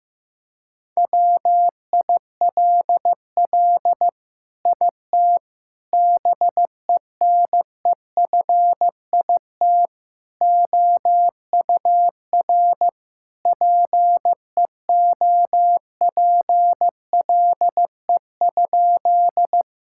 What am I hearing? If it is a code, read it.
Morse code, heard as a keyed tone, about 15 wpm, 700 Hz. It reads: WILL IT BENEFIT OUR PEOPLE?
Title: Sentences from Top 1000 Words - Review Part 6 - 15wpm